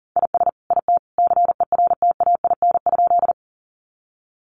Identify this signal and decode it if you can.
Morse code, heard as a keyed tone, about 40 wpm, 700 Hz. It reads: IS IT CERTAIN?
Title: Sentences from Top 500 Words 40wpm